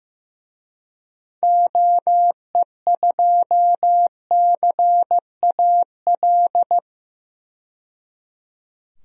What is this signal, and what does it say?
Morse code, heard as a keyed tone, about 15 wpm, 700 Hz. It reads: OE2CAL
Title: DXCC-100 Call Signs - 6 Characters - 15wpm